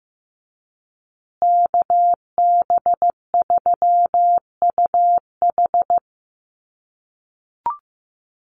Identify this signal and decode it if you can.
Morse code, heard as a keyed tone, about 15 wpm, 700 Hz. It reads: KB3UH
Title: USA Call Signs Repeated 3x 15wpm